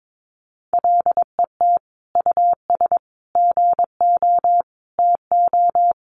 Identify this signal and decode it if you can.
Morse code, heard as a keyed tone, about 22 wpm, 700 Hz. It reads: LET US GO TO
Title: Sentences from Top 1500 Words 22wpm